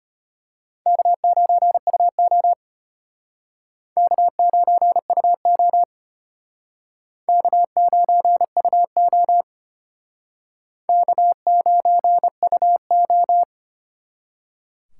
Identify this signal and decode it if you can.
Morse code, heard as a keyed tone, about 31 wpm, 700 Hz. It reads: K9UO K9UO K9UO K9UO
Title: Speed Racing - USA 1x2 Call Signs 25wpm